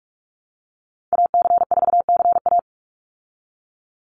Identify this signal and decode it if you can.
Morse code, heard as a keyed tone, about 44 wpm, 700 Hz. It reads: AC4CA
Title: Call Signs Repeated 3x - 45wpm